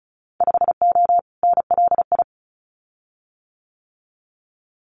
Morse code, heard as a keyed tone, about 34 wpm, 700 Hz. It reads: 5O NLI
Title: Winter Field Day Exchanges 35wpm